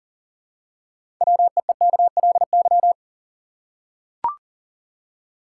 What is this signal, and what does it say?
Morse code, heard as a keyed tone, about 40 wpm, 700 Hz. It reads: WEEKLY
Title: Six Letter Words - 40wpm